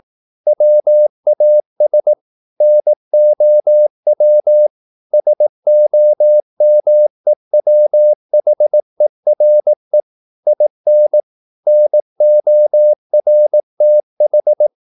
Morse code, heard as a keyed tone, about 18 wpm, 600 Hz. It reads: WAS NOW SOMEWHERE IN NORTH
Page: Romanen Never af Ken Follet på engelsk.